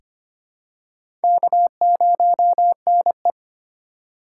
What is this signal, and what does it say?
Morse code, heard as a keyed tone, about 25 wpm, 700 Hz. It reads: K0NE
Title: USA Call Signs - 4 Characters - 4x Character Spacing - 25wpm